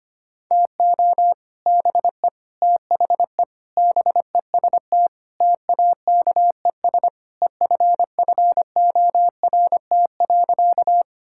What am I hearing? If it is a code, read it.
Morse code, heard as a keyed tone, about 25 wpm, 700 Hz. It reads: TO BE THE BEST TAKES EFFORT.
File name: practice / 25wpm / Sentences with 3-Word Phrases - 25wpm